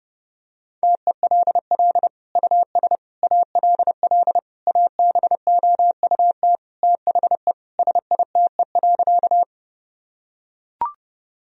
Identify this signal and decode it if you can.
Morse code, heard as a keyed tone, about 30 wpm, 700 Hz. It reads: TELL US ALL ABOUT THE SITE.
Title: Sentences from Top 1000 Words - Review Part 4 - 30wpm